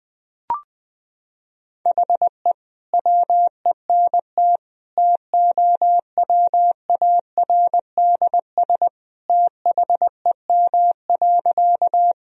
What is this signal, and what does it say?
Morse code, heard as a keyed tone, about 21 wpm, 700 Hz. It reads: HE WENT TOWARDS THEM.
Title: Sentences from Top 1000 Words - Review Part 2 - 20wpm